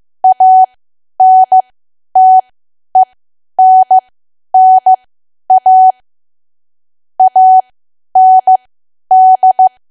Morse code, heard as a keyed tone, about 15 wpm, spaced as for 10 wpm, 750 Hz. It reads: ANTENNA AND
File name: Archive / 10 WPM / 10WPM